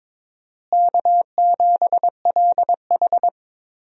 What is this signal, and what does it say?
Morse code, heard as a keyed tone, about 21 wpm, 700 Hz. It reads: K7LH
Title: USA Call Signs - 4 Characters - Repeated 2x - 22wpm